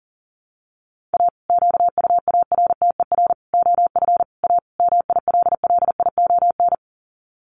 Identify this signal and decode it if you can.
Morse code, heard as a keyed tone, about 40 wpm, 700 Hz. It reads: A QUARTER OF A MILLION
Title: Sets of 5 Words Encore2 40wpm